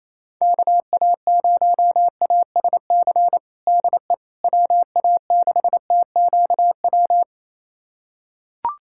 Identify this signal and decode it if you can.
Morse code, heard as a keyed tone, about 28 wpm, 700 Hz. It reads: KA0ASC DE WA6TQW
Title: USA Callsign DE Callsign - 6 Characters - 28wpm